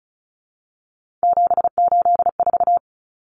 Morse code, heard as a keyed tone, about 34 wpm, 700 Hz. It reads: 784
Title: Common Cut Numbers - 3-Digit 35wpm